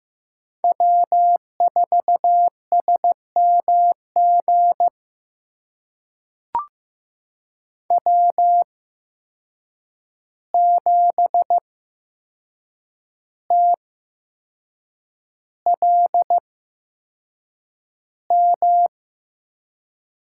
Morse code, heard as a keyed tone, about 15 wpm, spaced as for 6 wpm, 700 Hz. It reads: W4SMG W7TLM
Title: USA Call Signs - 5 Characters - 8x Character Spacing - 15wpm